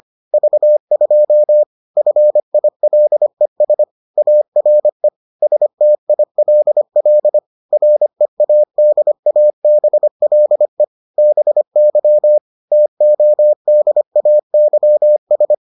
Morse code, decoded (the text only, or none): V2 FILES ARE STILL READABLE BY TODAYS